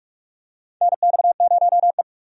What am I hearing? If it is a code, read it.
Morse code, heard as a keyed tone, about 43 wpm, 700 Hz. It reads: NX0E
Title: USA Call Signs - 4 Characters - 45wpm